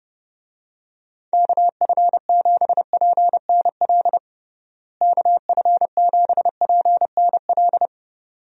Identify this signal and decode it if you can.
Morse code, heard as a keyed tone, about 32 wpm, 700 Hz. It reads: KF7PNL KF7PNL
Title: USA Call Signs - 6 Characters - Repeated 2x - 30wpm